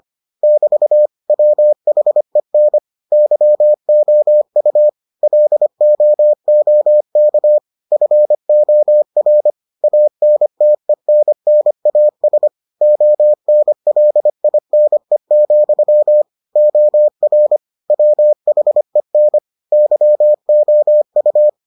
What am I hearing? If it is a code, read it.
Morse code, heard as a keyed tone, about 25 wpm, 600 Hz. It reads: <BT> WHEN YOU LOOK FOR ANTENNAS ONLINE, OR WHEN YOU